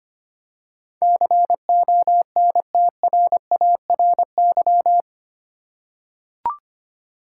Morse code, heard as a keyed tone, about 25 wpm, 700 Hz. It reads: CONTRARY
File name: Eight Letter Words - 25wpm